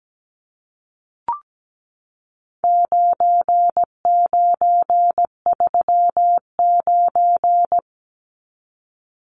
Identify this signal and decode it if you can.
Morse code, heard as a keyed tone, about 17 wpm, 700 Hz. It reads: 9939
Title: Numbers - 4-Digit 17wpm